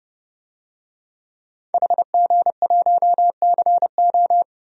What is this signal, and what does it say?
Morse code, heard as a keyed tone, about 30 wpm, 700 Hz. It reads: HG1CO
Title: DXCC-100 Call Signs - 5 Characters - 30wpm